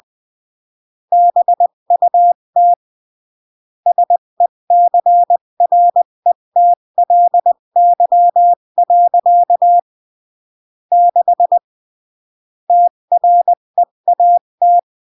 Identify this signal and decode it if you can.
Morse code, heard as a keyed tone, about 20 wpm, 700 Hz. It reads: BUT SECRETLY. 6 TREAT
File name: CW / jodorowsky / 82 commandments 20-15